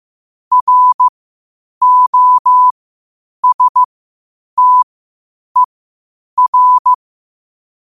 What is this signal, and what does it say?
Morse code, heard as a keyed tone, about 15 wpm, spaced as for 8 wpm, 1000 Hz. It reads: ROSTER
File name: qotdfiles / 8wpm